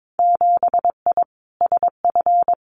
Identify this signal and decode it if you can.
Morse code, heard as a keyed tone, about 22 wpm, 700 Hz. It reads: 7I SF
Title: Winter Field Day Exchanges Encore 22wpm